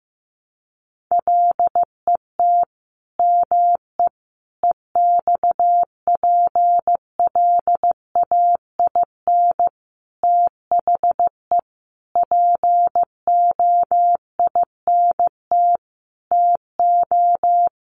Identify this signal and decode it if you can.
Morse code, heard as a keyed tone, about 15 wpm, 700 Hz. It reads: LET ME EXPLAIN THE POINT TO
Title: Sentences from Top 1100 Words 15wpm